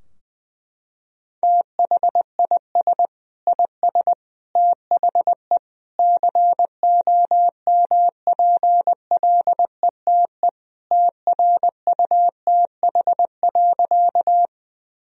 Morse code, heard as a keyed tone, about 19 wpm, 700 Hz. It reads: THIS IS THE COMPLETE TRUTH.
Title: Sentences from Top 1000 Words - Review Part 2 - 20wpm